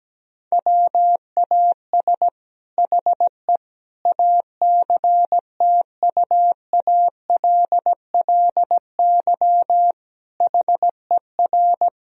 Morse code, heard as a keyed tone, about 17 wpm, 700 Hz. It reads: WAS HE ACTUALLY HER
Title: Sentences from Top 1000 Words - Review Part 4 - 17wpm